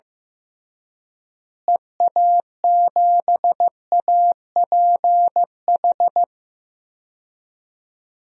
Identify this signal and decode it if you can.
Morse code, heard as a keyed tone, about 15 wpm, 700 Hz. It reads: EA7APH
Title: DXCC-100 Call Signs - 6 Characters - Repeated 3x - 15wpm